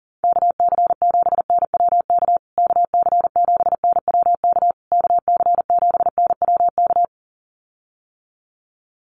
Morse code, heard as a keyed tone, about 40 wpm, 700 Hz. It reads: KC7NWK KC7NWK KC7NWK
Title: USA Call Signs Repeated 3x 40wpm